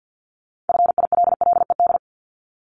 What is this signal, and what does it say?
Morse code, heard as a keyed tone, about 50 wpm, 700 Hz. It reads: FILLED
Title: Multi-letter Phonemes 50wpm